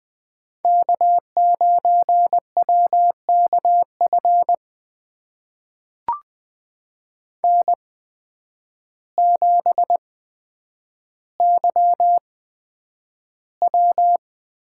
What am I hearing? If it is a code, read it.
Morse code, heard as a keyed tone, about 20 wpm, spaced as for 9 wpm, 700 Hz. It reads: K9WKF N7YW
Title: USA Call Signs - 5 Characters - 8x Character Spacing - 20wpm